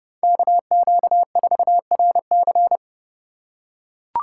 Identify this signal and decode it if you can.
Morse code, heard as a keyed tone, about 30 wpm, 700 Hz. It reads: KQ4RC
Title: USA Call Signs - 5 Characters - 2x Character Spacing - 30wpm